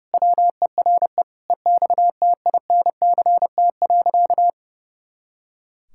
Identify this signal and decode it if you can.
Morse code, heard as a keyed tone, about 30 wpm, 700 Hz. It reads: WERE EXTINCT.
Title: Pangram Challenge v2 - 30wpm